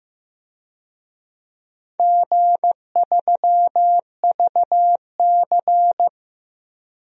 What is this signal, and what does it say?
Morse code, heard as a keyed tone, about 15 wpm, 700 Hz. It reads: G3VC
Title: DXCC-100 Call Signs - 4 Characters - Repeated 2x - 15wpm